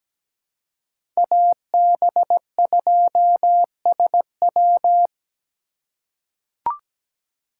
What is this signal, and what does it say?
Morse code, heard as a keyed tone, about 17 wpm, 700 Hz. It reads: AB2SW
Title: USA Call Signs - 5 Characters - 4x Character Spacing - 17wpm